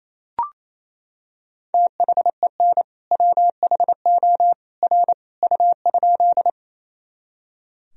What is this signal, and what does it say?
Morse code, heard as a keyed tone, about 28 wpm, 700 Hz. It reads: THEN WHO R U?
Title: Sentences from Top 100 Words and Abbreviations 28wpm